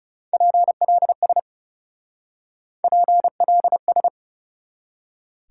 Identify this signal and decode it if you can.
Morse code, heard as a keyed tone, about 38 wpm, 700 Hz. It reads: PLS PLS